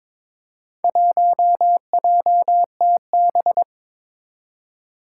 Morse code, heard as a keyed tone, about 22 wpm, 700 Hz. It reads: 1JTB
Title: Four Challenging Letter-Number Combinations 22wpm